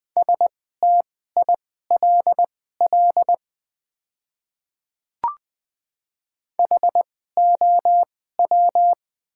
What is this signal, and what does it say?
Morse code, heard as a keyed tone, about 20 wpm, spaced as for 15 wpm, 700 Hz. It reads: STILL HOW
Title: 2x Character Spacing - Top 500 Words - 20wpm